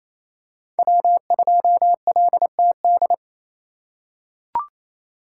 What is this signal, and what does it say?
Morse code, heard as a keyed tone, about 28 wpm, 700 Hz. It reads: W2LTD
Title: USA Call Signs - 5 Characters - 6x Character Spacing - 28wpm